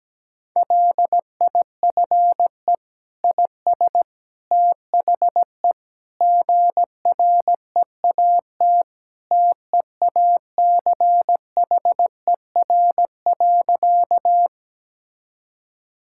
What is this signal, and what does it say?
Morse code, heard as a keyed tone, about 17 wpm, 700 Hz. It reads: LIFE IS THE GREAT TEACHER.
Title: Sentences from Top 1000 Words - Review Part 6 - 17wpm